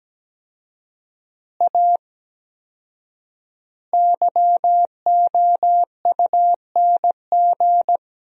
Morse code, heard as a keyed tone, about 17 wpm, 700 Hz. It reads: A YOUNG
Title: 4x Word-Spacing - Sentences from Top 400 Words - 17wpm